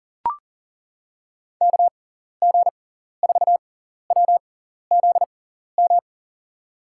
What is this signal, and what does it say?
Morse code, heard as a keyed tone, about 40 wpm, spaced as for 14 wpm, 700 Hz. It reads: KG4WZM
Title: USA Call Signs - 6 Characters - 6x Character Spacing - 40wpm